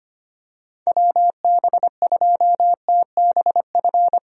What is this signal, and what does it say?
Morse code, heard as a keyed tone, about 25 wpm, 700 Hz. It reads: WB2TBF
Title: USA Call Signs - 6 Characters - 25wpm